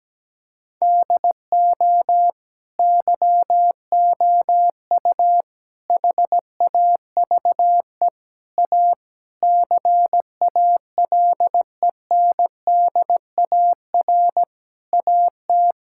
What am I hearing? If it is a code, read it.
Morse code, heard as a keyed tone, about 17 wpm, 700 Hz. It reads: DO YOU HAVE A CALENDAR AT
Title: Sentences from Top 1600 Words 17wpm